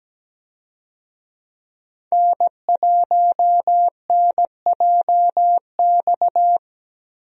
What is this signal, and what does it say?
Morse code, heard as a keyed tone, about 17 wpm, 700 Hz. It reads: N1NJX
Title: USA Call Signs - 5 Characters - Repeated 2x - 17wpm